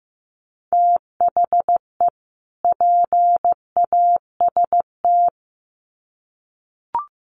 Morse code, heard as a keyed tone, about 15 wpm, 700 Hz. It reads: THE PAST
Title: Sets of 2 Words 15wpm